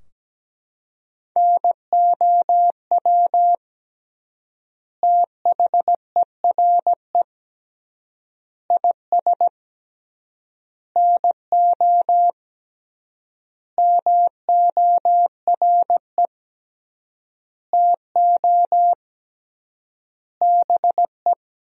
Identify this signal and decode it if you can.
Morse code, heard as a keyed tone, about 17 wpm, 700 Hz. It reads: NOW THERE IS NO MORE TO BE
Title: 3x Word-Spacing - Sentences from Top 100 Words - 17wpm